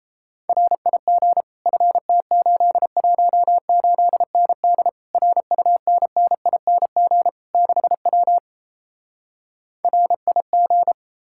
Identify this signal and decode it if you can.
Morse code, heard as a keyed tone, about 38 wpm, 700 Hz. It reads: RIG FT818ND RUNNING 6W RIG